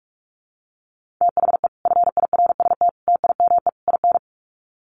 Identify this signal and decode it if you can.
Morse code, heard as a keyed tone, about 44 wpm, 700 Hz. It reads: THE FIRST TIME IN
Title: Sets of 4 Words 45wpm